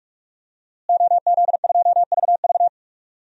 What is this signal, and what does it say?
Morse code, heard as a keyed tone, about 46 wpm, 700 Hz. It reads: OZ2VV
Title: DXCC-100 Call Signs - 5 Characters - Repeated 2x - 45wpm